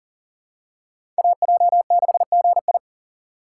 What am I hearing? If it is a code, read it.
Morse code, heard as a keyed tone, about 40 wpm, 700 Hz. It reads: AJ6GI
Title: USA Call Signs - 5 Characters - 4x Character Spacing - 40wpm